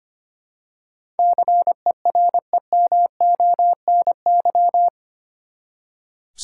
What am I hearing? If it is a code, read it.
Morse code, heard as a keyed tone, about 25 wpm, 700 Hz. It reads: CEREMONY
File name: Eight Letter Words - 25wpm